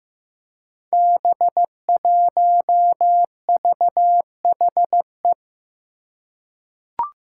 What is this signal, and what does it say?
Morse code, heard as a keyed tone, about 15 wpm, 700 Hz. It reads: B1VHE
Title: Five Challenging Letter-Number Combinations 15wpm